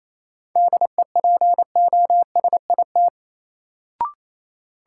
Morse code, heard as a keyed tone, about 28 wpm, 700 Hz. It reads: DEPOSIT
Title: Sentences from Top 1300 Words - New Word Drill - 28wpm